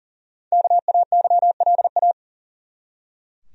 Morse code, heard as a keyed tone, about 40 wpm, 700 Hz. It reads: KAYLA
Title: USA Common Names - Max 5 Letters - 40wpm